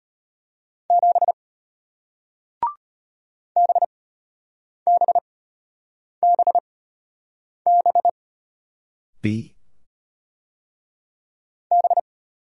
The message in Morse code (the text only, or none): 7 BBBB B